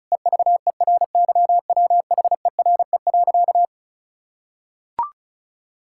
Morse code, heard as a keyed tone, about 36 wpm, 700 Hz. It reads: EVERYWHERE.
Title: Sentences from Top 1600 Words 35wpm